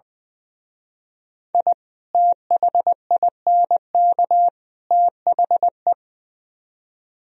Morse code, heard as a keyed tone, about 23 wpm, 700 Hz. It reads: I THINK THE
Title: Speed-Racing - Sets of 3 Words 20wpm